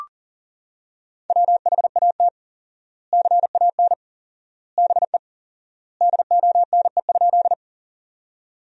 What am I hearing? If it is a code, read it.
Morse code, heard as a keyed tone, about 37 wpm, 700 Hz. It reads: WHAT CAN BE DONE?
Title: 4x Word-Spacing - Sentences from Top 500 Words - 40wpm